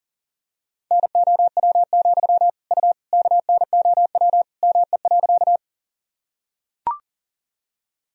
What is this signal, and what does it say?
Morse code, heard as a keyed tone, about 40 wpm, 700 Hz. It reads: NOW, U KNOW ME.
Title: Sentences from Top 200 Words and Abbreviations 40wpm